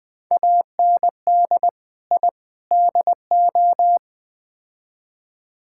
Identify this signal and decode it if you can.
Morse code, heard as a keyed tone, about 20 wpm, 700 Hz. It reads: AND I DO
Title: Build A Phrase - 4-Word Phrases - 20wpm